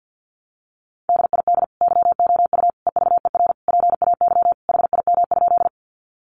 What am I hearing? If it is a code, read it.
Morse code, heard as a keyed tone, about 48 wpm, 700 Hz. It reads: DID YOU EVER PAY HIM?
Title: Sentences from Top 700 Words 50wpm